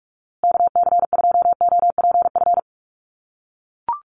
Morse code, heard as a keyed tone, about 44 wpm, 700 Hz. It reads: KC2OPF
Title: USA Call Signs Repeated 2x 45wpm